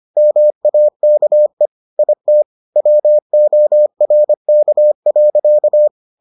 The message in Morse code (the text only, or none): MAKE IT WORK.